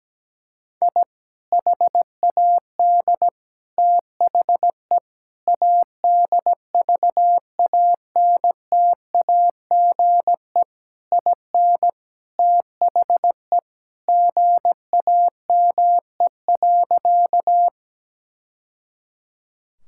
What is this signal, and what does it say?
Morse code, heard as a keyed tone, about 17 wpm, 700 Hz. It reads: I HAD THE ADVANTAGE IN THE GAME.
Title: Sentences from Top 1000 Words - Review Part 6 - 17wpm